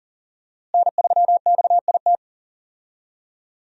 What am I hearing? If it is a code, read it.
Morse code, heard as a keyed tone, about 40 wpm, 690 Hz. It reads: N3XIT
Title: USA Call Signs - 5 Characters - Repeated 3x - 40wpm